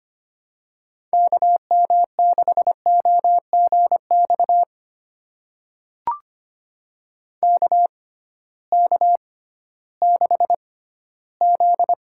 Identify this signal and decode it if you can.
Morse code, heard as a keyed tone, about 25 wpm, spaced as for 14 wpm, 700 Hz. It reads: KM6OGX KK6Z
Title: USA Call Signs - 6 Characters - 6x Character Spacing - 25wpm